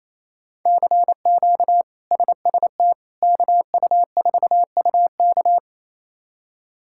Morse code, heard as a keyed tone, about 28 wpm, 700 Hz. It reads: CQ SST KU4UK
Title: K1USN SST QSO Practice - 28wpm